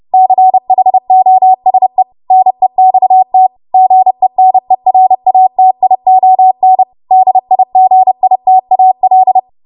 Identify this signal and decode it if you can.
Morse code, heard as a keyed tone, about 30 wpm, 750 Hz. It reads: CHOSE NEXT GENERATION DIGITAL
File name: Morse / 30WPM